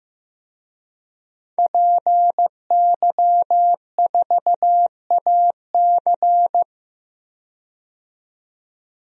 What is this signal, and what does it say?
Morse code, heard as a keyed tone, about 16 wpm, 700 Hz. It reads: PY4AC
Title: DXCC-100 Call Signs - 5 Characters - 15wpm